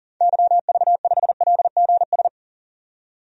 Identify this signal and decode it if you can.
Morse code, heard as a keyed tone, about 38 wpm, 700 Hz. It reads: YV5LGS
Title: DXCC-100 Call Signs - 6 Characters - 40wpm